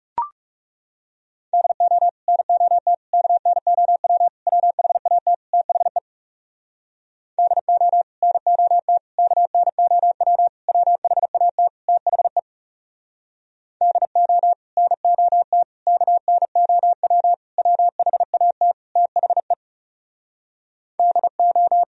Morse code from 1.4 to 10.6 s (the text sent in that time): DO NOT KNOW WHAT THE DO NOT KNOW